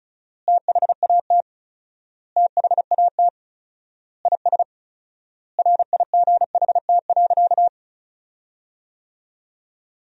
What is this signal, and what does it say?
Morse code, heard as a keyed tone, about 36 wpm, 700 Hz. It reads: THAT THAT IS RIGHT.
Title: 4x Word-Spacing - Sentences from Top 200 Words - 35wpm